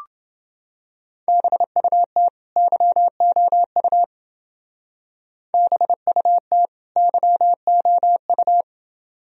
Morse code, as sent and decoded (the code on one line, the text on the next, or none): -... ..- - / -.-- --- ..- / -... ..- - / -.-- --- ..-
BUT YOU BUT YOU